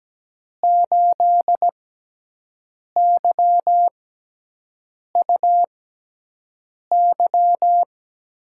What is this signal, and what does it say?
Morse code, heard as a keyed tone, about 17 wpm, spaced as for 6 wpm, 700 Hz. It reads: 8YUY
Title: USA Call Signs - 5 Characters - 6x Character Spacing - 17wpm